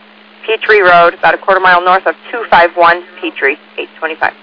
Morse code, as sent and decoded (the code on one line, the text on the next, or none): none